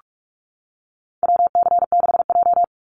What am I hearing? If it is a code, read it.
Morse code, heard as a keyed tone, about 46 wpm, 700 Hz. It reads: WC6J
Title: NA SOTA Activator Callsigns 45wpm